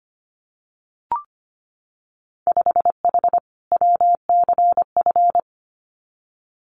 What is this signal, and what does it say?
Morse code, heard as a keyed tone, about 25 wpm, 700 Hz. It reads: 5H WCF
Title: Winter Field Day Exchanges Encore 25wpm